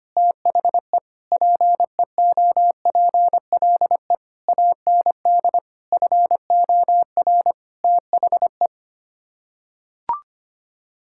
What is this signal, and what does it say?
Morse code, heard as a keyed tone, about 25 wpm, 700 Hz. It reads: THE PEOPLE AND FOR THE
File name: Sets of 9 Words - 25wpm